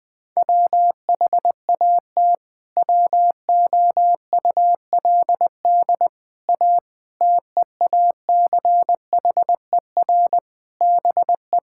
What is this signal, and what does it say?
Morse code, heard as a keyed tone, about 20 wpm, 700 Hz. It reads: WHAT WOULD A TEACHER BE
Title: Sentences from Top 1000 Words - Review Part 6 - 20wpm